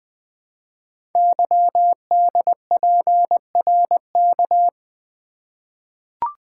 Morse code, heard as a keyed tone, about 20 wpm, 700 Hz. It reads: YDPRK
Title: Code Groups - 5 Letters - 20wpm